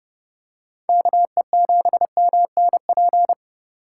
Morse code, heard as a keyed tone, about 30 wpm, 700 Hz. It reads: KE7MNP